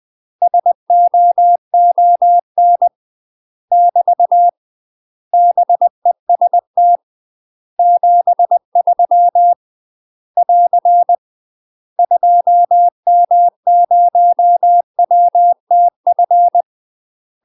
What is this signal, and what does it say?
Morse code, heard as a keyed tone, about 20 wpm, 700 Hz. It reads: SOON <BT> BEST 73 <AR> 2M0WTF
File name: UK MASTER QSOs TRAINING / 20WPM / UK MASTER QSOs 0046